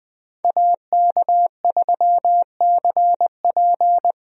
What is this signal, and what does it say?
Morse code, heard as a keyed tone, about 21 wpm, 700 Hz. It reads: AK3CP
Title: USA Call Signs - 5 Characters - Repeated 2x - 20wpm